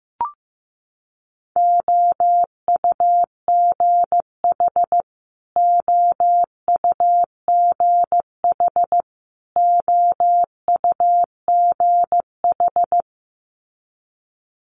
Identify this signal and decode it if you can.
Morse code, heard as a keyed tone, about 15 wpm, 700 Hz. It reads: OUGH OUGH OUGH
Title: Multi-letter Phonemes Repeated 3x 15wpm